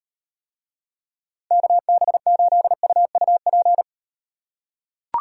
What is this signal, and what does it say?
Morse code, heard as a keyed tone, about 38 wpm, 700 Hz. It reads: KB8UUP